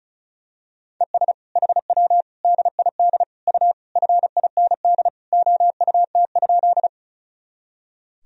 Morse code, heard as a keyed tone, about 36 wpm, 700 Hz. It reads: ES HW DID U FIND OUT?
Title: Sentences from Top 100 Words and Abbreviations 35wpm